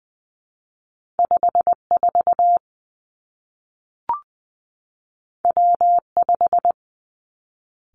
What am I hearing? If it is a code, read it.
Morse code, heard as a keyed tone, about 20 wpm, 700 Hz. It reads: 5 4 W 5
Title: Two Challenging Letter-Number Combinations 20wpm